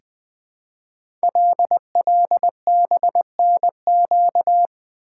Morse code, heard as a keyed tone, about 20 wpm, 700 Hz. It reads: LLBNQ
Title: Code Groups - 5 Letters - 20wpm